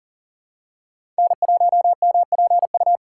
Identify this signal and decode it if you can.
Morse code, heard as a keyed tone, about 40 wpm, 700 Hz. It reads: N1MPU
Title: USA Call Signs - 5 Characters - 3x Character Spacing - 40wpm